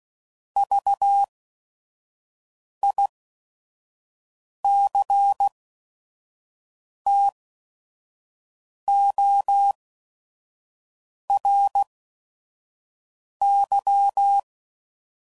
Morse code, heard as a keyed tone, about 16 wpm, spaced as for 5 wpm, 800 Hz. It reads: VICTORY
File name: Prefixes / By First Letter / Vs